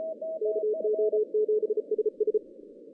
Morse code, heard as a keyed tone, about 34 wpm, 430 Hz, with radio noise. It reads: KO7SS
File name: 2015 cq ww cw / 2015 cqwww k5tr - 0-18